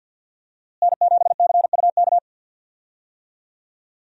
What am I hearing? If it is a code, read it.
Morse code, heard as a keyed tone, about 51 wpm, 700 Hz. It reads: N7CUK
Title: USA Call Signs - 5 Characters - Repeated 2x - 50wpm